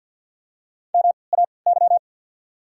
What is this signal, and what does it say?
Morse code, heard as a keyed tone, about 53 wpm, spaced as for 29 wpm, 700 Hz. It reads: MAY
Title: 3x Character Spacing - Top 500 Words - 50wpm